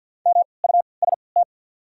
Morse code, heard as a keyed tone, about 53 wpm, 690 Hz. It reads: M U S T
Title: 3x Character Spacing - Top 200 Words - 50wpm